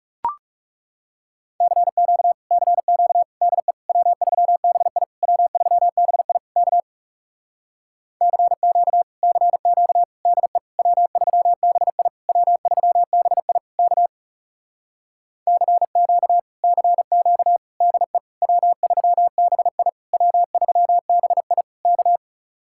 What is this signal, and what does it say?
Morse code, heard as a keyed tone, about 40 wpm, 700 Hz. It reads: CQ CQ DE W3BI W3BI K CQ CQ DE W3BI W3BI K CQ CQ DE W3BI W3BI K